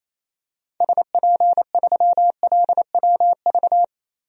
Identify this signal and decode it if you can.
Morse code, heard as a keyed tone, about 28 wpm, 700 Hz. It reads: SP3LWV
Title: DXCC-100 Call Signs - 6 Characters - Repeated 2x - 28wpm